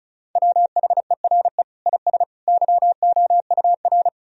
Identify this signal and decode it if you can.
Morse code, heard as a keyed tone, about 36 wpm, 700 Hz. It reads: WHERE IS YOUR
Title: Sentences from Top 1000 Words - Review Part 1 - 35wpm